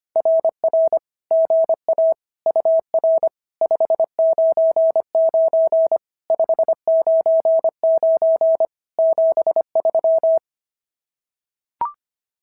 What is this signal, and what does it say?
Morse code, heard as a keyed tone, about 25 wpm, 650 Hz. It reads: RR GA UR 599 599 73
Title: SOTA QSO Practice 25wpm